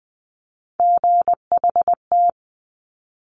Morse code, heard as a keyed tone, about 20 wpm, 700 Hz. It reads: GHT
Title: Top Three Letter Combinations in Words 20wpm